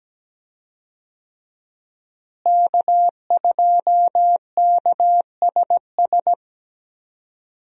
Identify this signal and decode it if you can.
Morse code, heard as a keyed tone, about 17 wpm, 700 Hz. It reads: K2KSS
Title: USA Call Signs - 5 Characters - Repeated 2x - 17wpm